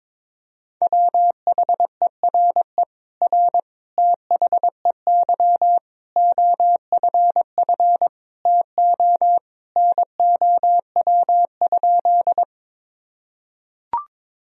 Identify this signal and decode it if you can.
Morse code, heard as a keyed tone, about 22 wpm, 700 Hz. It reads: WHERE R THEY OFF TO NOW?
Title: Sentences from Top 200 Words and Abbreviations 22wpm